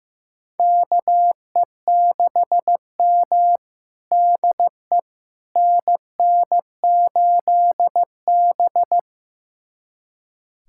Morse code, heard as a keyed tone, about 16 wpm, 700 Hz. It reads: KE6M DE NN8B
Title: USA Callsign DE Callsign - 4 Characters - 15wpm